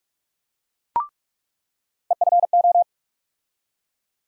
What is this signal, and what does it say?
Morse code, heard as a keyed tone, about 41 wpm, 700 Hz. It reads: EFO
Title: USA Call Sign Postfixes - 3 Letters 45wpm